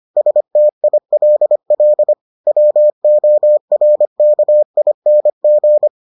Morse code, heard as a keyed tone, about 25 wpm, 600 Hz. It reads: STILL WORKING